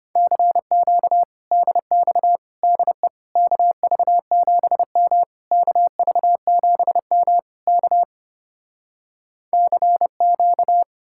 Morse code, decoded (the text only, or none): CQ DX DE KV7M KV7M K CQ